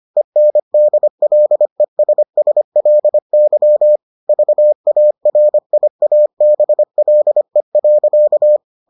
Morse code, heard as a keyed tone, about 25 wpm, 600 Hz. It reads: ENDLESSLY VARIABLE.